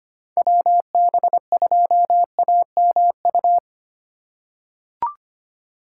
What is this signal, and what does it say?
Morse code, heard as a keyed tone, about 25 wpm, 700 Hz. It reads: WB2AMU
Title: USA Call Signs - 6 Characters - 3x Character Spacing - 25wpm